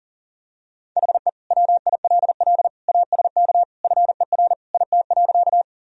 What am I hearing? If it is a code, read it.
Morse code, heard as a keyed tone, about 40 wpm, 700 Hz. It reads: HE WILL ASK FER IT.
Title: Sentences from Top 200 Words and Abbreviations 40wpm